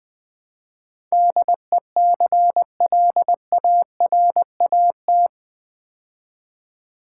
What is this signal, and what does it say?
Morse code, heard as a keyed tone, about 20 wpm, 700 Hz. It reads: DECLARAT
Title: Build A Word - 4 Letters At A Time - 11-Letter Words - 20wpm